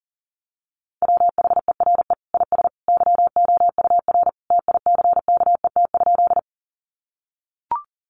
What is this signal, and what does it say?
Morse code, heard as a keyed tone, about 40 wpm, 700 Hz. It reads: WHERE IS YOUR TICKET?
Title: Sentences from Top 1400 Words 40wpm